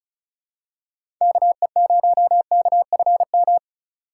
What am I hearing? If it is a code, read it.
Morse code, heard as a keyed tone, about 36 wpm, 700 Hz. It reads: KE0KFM
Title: USA Call Signs - 6 Characters - 3x Character Spacing - 35wpm